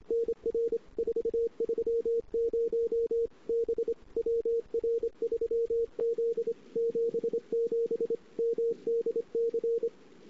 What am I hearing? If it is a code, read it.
Morse code, heard as a keyed tone, about 26 wpm, 460 Hz. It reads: NR 430 B WR3Z 77 MDC
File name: CW / 2008-ss-cw-k5tr-op-n5rz -03:00